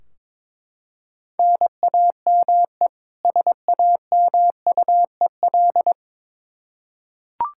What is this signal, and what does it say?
Morse code, heard as a keyed tone, about 21 wpm, 700 Hz. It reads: NAME SAMUEL
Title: Practice QSO - Exchange 1 - Beginning - 22wpm